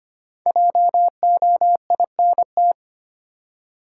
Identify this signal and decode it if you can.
Morse code, heard as a keyed tone, about 25 wpm, 700 Hz. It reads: JOINT
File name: 5 Letter Words - First 3 Letters Given - 25wpm